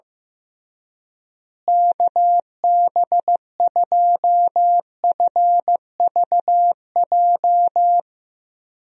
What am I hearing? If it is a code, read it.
Morse code, heard as a keyed tone, about 15 wpm, 700 Hz. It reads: KB2FVJ
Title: USA Call Signs - 6 Characters - Repeated 2x - 15wpm